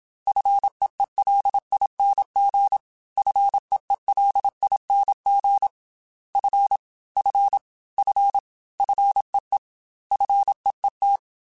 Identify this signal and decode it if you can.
Morse code, heard as a keyed tone, about 27 wpm, 800 Hz. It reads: FEELING FEELING F F F FEE FEET
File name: F and L / F and L mp3 / F and L Prefixes 3x2 Twice @25